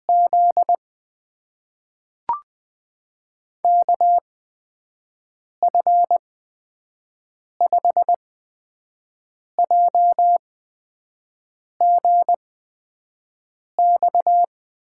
Morse code, heard as a keyed tone, about 20 wpm, spaced as for 5 wpm, 700 Hz. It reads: Z KF5JGX
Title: USA Call Signs - 6 Characters - 8x Character Spacing - 20wpm